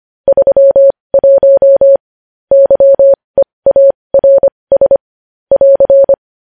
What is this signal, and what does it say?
Morse code, heard as a keyed tone, about 25 wpm, spaced as for 20 wpm, 550 Hz. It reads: 31 YEARS <AR>
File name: podcast / News headlines-1x-25x20